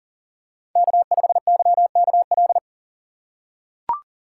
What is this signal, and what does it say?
Morse code, heard as a keyed tone, about 40 wpm, 700 Hz. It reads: K5YKL
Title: USA Call Signs - 3x Character Spacing - 40wpm